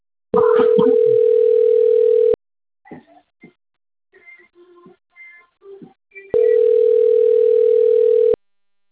Ringback tone (North America), 440 + 480 Hz.